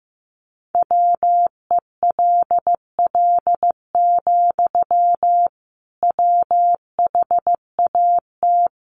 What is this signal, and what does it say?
Morse code, heard as a keyed tone, about 15 wpm, 700 Hz. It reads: WELL, WHAT
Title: Sentences from Top 500 Words 15wpm